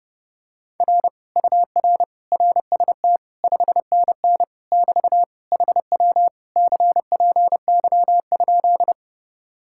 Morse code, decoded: R UR RST 5NN <BT> HW CPY?

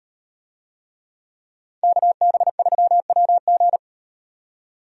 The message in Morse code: KB3WG

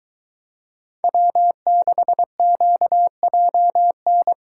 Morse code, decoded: W6QJN

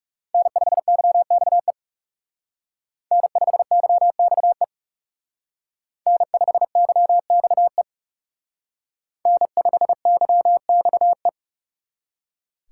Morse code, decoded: N5YXE N5YXE N5YXE N5YXE

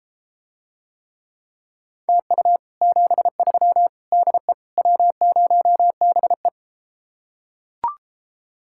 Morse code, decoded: TU 73 DE W0BE

